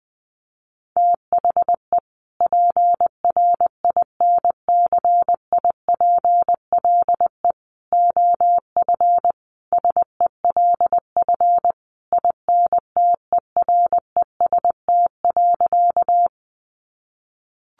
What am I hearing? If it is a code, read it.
Morse code, heard as a keyed tone, about 20 wpm, 700 Hz. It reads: THE PRINCIPLE OF SELF INTEREST.
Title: Sentences from Top 1400 Words 20wpm